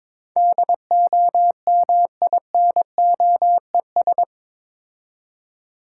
Morse code, heard as a keyed tone, about 26 wpm, 700 Hz. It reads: DOMINOES
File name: Easy Words - Speed-Racing with Hints - 22wpm